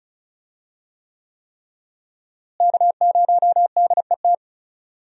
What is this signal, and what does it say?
Morse code, heard as a keyed tone, about 36 wpm, 700 Hz. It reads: K0DET
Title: USA Call Signs - 5 Characters - Repeated 3x - 35wpm